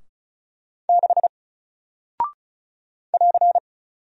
Morse code, heard as a keyed tone, about 36 wpm, 700 Hz. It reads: 6 <AR>
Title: Single Character (Letter:Number:Punctuation:Prosign) Alt 35wpm